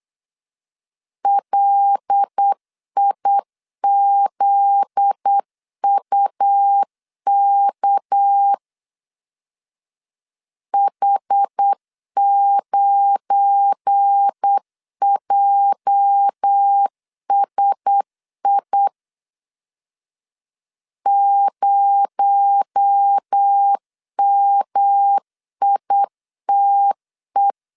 Morse code, heard as a keyed tone, about 8 wpm, 800 Hz. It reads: LIZUK H9JSI 0MITE
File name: LEZIONI / 8 YJ45 / 8 YJ 8wpm 3313